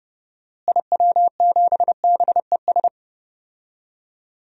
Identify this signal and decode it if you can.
Morse code, heard as a keyed tone, about 30 wpm, 700 Hz. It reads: IW7BES
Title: DXCC-100 Call Signs - 6 Characters - Repeated 3x - 30wpm